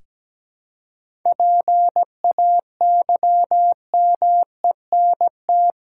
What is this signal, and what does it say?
Morse code, heard as a keyed tone, about 17 wpm, 700 Hz. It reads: PAYMENT
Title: Seven Letter Words - 17wpm